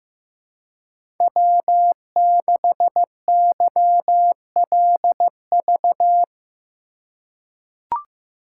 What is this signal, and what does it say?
Morse code, heard as a keyed tone, about 15 wpm, 700 Hz. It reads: W6YLV
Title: USA Call Signs - 5 Characters - 3x Character Spacing - 15wpm